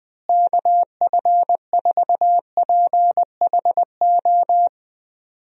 Morse code, heard as a keyed tone, about 20 wpm, 700 Hz. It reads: KF4PHO
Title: USA Call Signs - 6 Characters - Repeated 3x - 20wpm